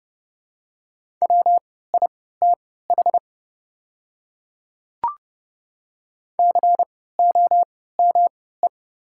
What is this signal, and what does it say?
Morse code, heard as a keyed tone, about 30 wpm, spaced as for 17 wpm, 700 Hz. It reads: WITH COME
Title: 3x Character Spacing - Top 100 Words - 30wpm